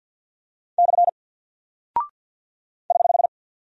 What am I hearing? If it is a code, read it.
Morse code, heard as a keyed tone, about 48 wpm, 700 Hz. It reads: / <HH>
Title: Single Character (Letter:Number:Punctuation:Prosign) Alt 50wpm